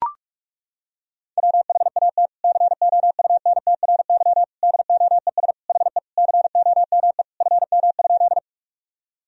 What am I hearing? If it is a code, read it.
Morse code, heard as a keyed tone, about 41 wpm, 700 Hz. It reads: WHAT COUNTRY DOES HE COME FM?
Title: Sentences from Top 300 Words and Abbreviations 45wpm